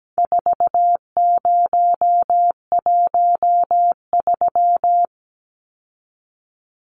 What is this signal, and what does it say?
Morse code, heard as a keyed tone, about 17 wpm, 700 Hz. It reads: 4013
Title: Numbers - 4-Digit 17wpm